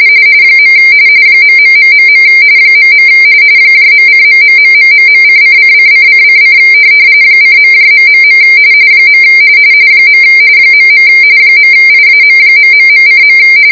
RTTY, about 45.45 baud. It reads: . ..... .. .:......:::::::: :.... . .. .IAM...:.